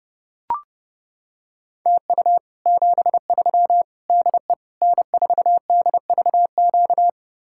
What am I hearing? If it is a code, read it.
Morse code, heard as a keyed tone, about 31 wpm, 700 Hz. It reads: TU 73 DE N4DVQ